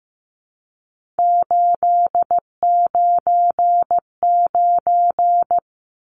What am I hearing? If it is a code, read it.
Morse code, heard as a keyed tone, about 15 wpm, 700 Hz. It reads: 899
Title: Numbers - 3-Digit 15wpm